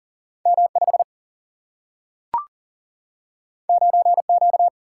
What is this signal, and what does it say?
Morse code, heard as a keyed tone, about 40 wpm, 700 Hz. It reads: M5 9Q
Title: Single-Letter Single-Number 40wpm